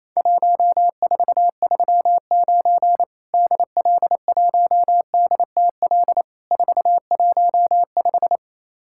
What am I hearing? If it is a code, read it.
Morse code, heard as a keyed tone, about 28 wpm, 700 Hz. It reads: 1439 DL1DTL 415
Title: Worked All Europe DX Contest - QTC Exchange - 28wpm